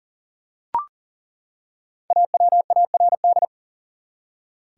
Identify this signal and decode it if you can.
Morse code, heard as a keyed tone, about 40 wpm, 700 Hz. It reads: AWARD
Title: Five Letter Words - 40wpm